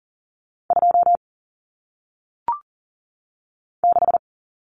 Morse code, heard as a keyed tone, about 39 wpm, 700 Hz. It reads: 2 6
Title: Numbers - 1-Digit 40wpm